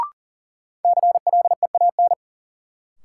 Morse code, heard as a keyed tone, about 37 wpm, 700 Hz. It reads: CLEAN
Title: Sentences from Top 1100 Words - New Word Drill Alt - 40wpm